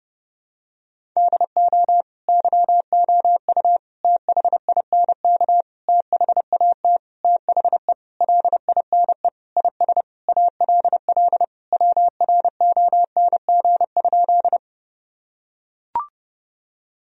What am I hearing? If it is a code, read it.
Morse code, heard as a keyed tone, about 30 wpm, 700 Hz. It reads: DO YOU THINK THAT THE LINE IS ALL WRONG?
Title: Sentences from Top 1000 Words - Review Part 1 - 30wpm